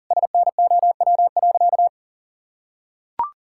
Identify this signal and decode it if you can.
Morse code, heard as a keyed tone, about 38 wpm, 700 Hz. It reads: SNOW.
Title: Sentences from Top 1000 Words - Review Part 5 - 40wpm